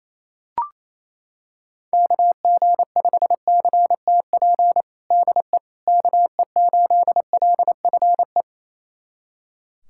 Morse code, heard as a keyed tone, about 28 wpm, 700 Hz. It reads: KG5CTP DE KE8LFE
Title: USA Callsign DE Callsign - 6 Characters - 28wpm